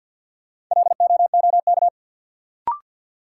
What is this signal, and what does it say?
Morse code, heard as a keyed tone, about 52 wpm, 700 Hz. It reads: LOOK
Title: Sets of 2 Words - New Word Drill - 50wpm